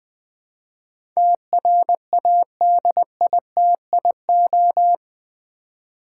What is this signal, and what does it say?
Morse code, heard as a keyed tone, about 20 wpm, 700 Hz. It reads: TRADITIO
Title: Build A Word - 4 Letters At A Time - 11-Letter Words - 20wpm